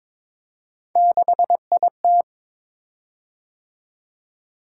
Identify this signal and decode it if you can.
Morse code, heard as a keyed tone, about 22 wpm, 700 Hz. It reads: 6IT
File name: practice / 22wpm / Three Letter-Number Combinations - 22wpm